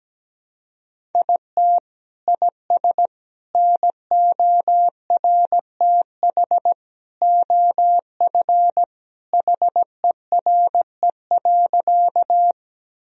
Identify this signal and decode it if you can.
Morse code, heard as a keyed tone, about 17 wpm, 700 Hz. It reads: IT IS NORTH OF HERE.
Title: Sentences from Top 1000 Words - Review Part 1 - 17wpm